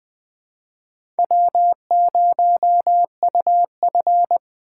Morse code, heard as a keyed tone, about 21 wpm, 700 Hz. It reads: W0UF
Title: K1USN SST QSO Practice - 20wpm